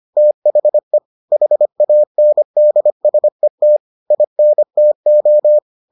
Morse code, heard as a keyed tone, about 25 wpm, 600 Hz. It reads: THE HANDSET INTO